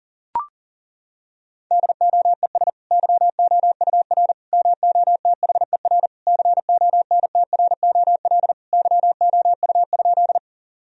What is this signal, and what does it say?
Morse code, heard as a keyed tone, about 40 wpm, 700 Hz. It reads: DOES YOUR MOTHER CONTROL YOU?
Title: Sentences from Top 1000 Words - Review Part 3 - 40wpm